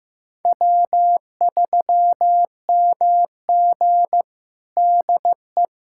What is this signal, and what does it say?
Morse code, heard as a keyed tone, about 15 wpm, 700 Hz. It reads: W3MG DE
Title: USA Callsign DE Callsign - 4 Characters - 15wpm